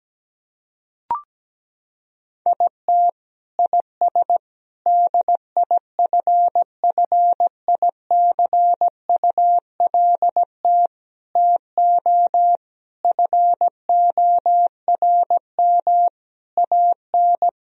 Morse code, heard as a keyed tone, about 17 wpm, 700 Hz. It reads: IT IS DIFFICULT TO FORM AN
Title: Sentences from Top 1000 Words - Review Part 1 - 17wpm